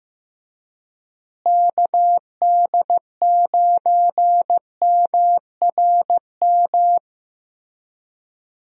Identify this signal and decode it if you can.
Morse code, heard as a keyed tone, about 16 wpm, 700 Hz. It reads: KD9MRM
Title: USA Call Signs - 6 Characters - Repeated 2x - 15wpm